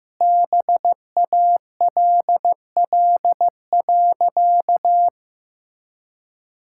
Morse code, heard as a keyed tone, about 15 wpm, 700 Hz. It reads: BALL.
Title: Sentences from Top 1000 Words - Review Part 6 - 15wpm